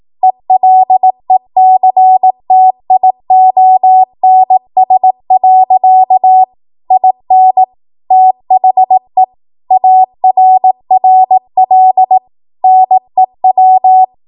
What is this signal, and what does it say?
Morse code, heard as a keyed tone, about 18 wpm, 750 Hz. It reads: ELECTIONS. IN THE ARRL NEW